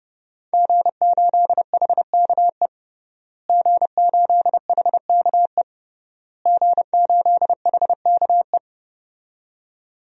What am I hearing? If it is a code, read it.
Morse code, heard as a keyed tone, about 31 wpm, 700 Hz. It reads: G8HKE G8HKE G8HKE
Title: DXCC-100 Call Signs - 5 Characters - Repeated 3x - 30wpm